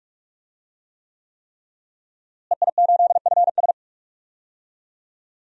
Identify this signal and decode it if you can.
Morse code, heard as a keyed tone, about 46 wpm, 700 Hz. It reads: EI8FS
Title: DXCC-100 Call Signs - 5 Characters - Repeated 3x - 45wpm